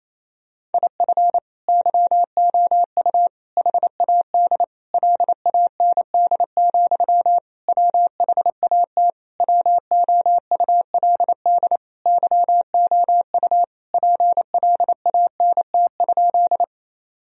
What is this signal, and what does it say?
Morse code, heard as a keyed tone, about 27 wpm, 700 Hz. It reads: IF YOU HAD LAND, WHAT WOULD YOU PLANT?
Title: Sentences from Top 1000 Words - Review Part 1 - 28wpm